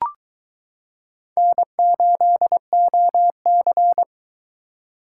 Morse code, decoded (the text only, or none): N8OC